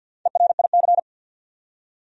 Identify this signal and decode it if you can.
Morse code, heard as a keyed tone, about 52 wpm, 700 Hz. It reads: ERIC
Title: USA Common Names - Max 5 Letters - 50wpm